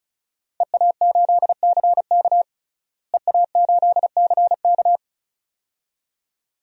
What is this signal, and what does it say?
Morse code, heard as a keyed tone, about 36 wpm, 700 Hz. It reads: EA8CK EA8CK
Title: DXCC-100 Call Signs - 5 Characters - Repeated 2x - 35wpm